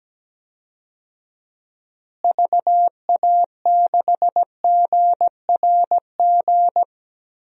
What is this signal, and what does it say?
Morse code, heard as a keyed tone, about 17 wpm, 700 Hz. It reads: VA6GRG
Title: DXCC-100 Call Signs - 6 Characters - Repeated 3x - 17wpm